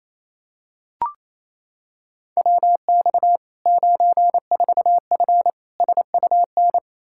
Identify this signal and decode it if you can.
Morse code, heard as a keyed tone, about 28 wpm, 700 Hz. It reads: WX 94F SUN
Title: Practice QSO - Exchange 2 - Info - 28wpm